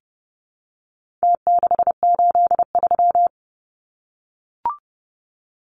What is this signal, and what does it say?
Morse code, heard as a keyed tone, about 30 wpm, 700 Hz. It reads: T683
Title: Numbers - Contest Style - 4-Digit 30wpm